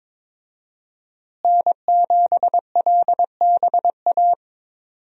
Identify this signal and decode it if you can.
Morse code, heard as a keyed tone, about 22 wpm, 700 Hz. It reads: N7LBA
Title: USA Call Signs - 5 Characters - Repeated 2x - 22wpm